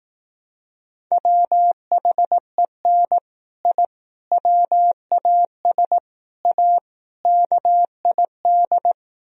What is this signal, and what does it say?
Morse code, heard as a keyed tone, about 18 wpm, 700 Hz. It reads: WHEN I WAS A KID
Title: Speed-Racing - Sets of 5 Words 15wpm